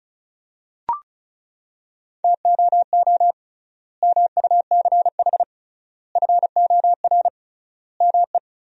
Morse code, heard as a keyed tone, about 34 wpm, 700 Hz. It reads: TOO MUCH FOR ME
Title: 3x Word-Spacing - Sentences from Top 200 Words - 35wpm